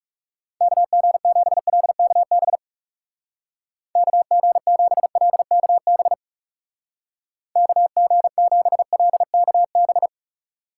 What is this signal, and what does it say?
Morse code, heard as a keyed tone, about 37 wpm, 700 Hz. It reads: KG7LKB KG7LKB KG7LKB
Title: Speed Racing - USA 2x3 Call Signs 30wpm